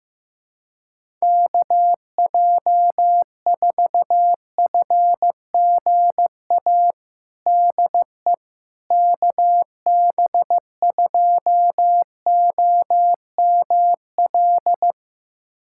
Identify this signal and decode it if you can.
Morse code, heard as a keyed tone, about 15 wpm, 700 Hz. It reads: KJ4FGA DE KB2OML
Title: USA Callsign DE Callsign - 6 Characters - 15wpm